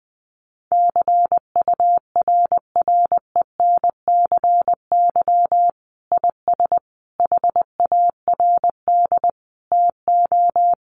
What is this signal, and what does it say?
Morse code, heard as a keyed tone, about 20 wpm, 700 Hz. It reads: CURRENCY IS HARD TO
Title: Sentences from Top 1500 Words 20wpm